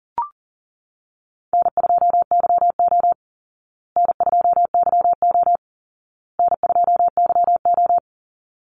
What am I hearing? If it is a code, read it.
Morse code, heard as a keyed tone, about 40 wpm, 700 Hz. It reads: N2YO N2YO N2YO
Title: Easy Call Signs Repeated 3x - 40wpm